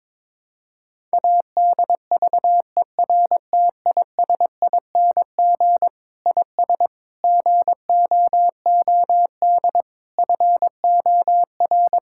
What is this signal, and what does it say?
Morse code, heard as a keyed tone, about 22 wpm, 700 Hz. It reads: ADVERTISING IS GOOD FOR
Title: Sentences from Top 1000 Words - Review Part 6 - 22wpm